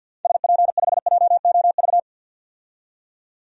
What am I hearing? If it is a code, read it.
Morse code, heard as a keyed tone, about 53 wpm, 700 Hz. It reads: SP5JOV
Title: DXCC-100 Call Signs - 6 Characters - 50wpm